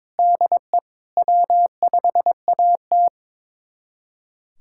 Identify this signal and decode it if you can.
Morse code, heard as a keyed tone, about 22 wpm, 700 Hz. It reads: DE W5AT
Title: USA Callsign DE Callsign - 4 Characters - 22wpm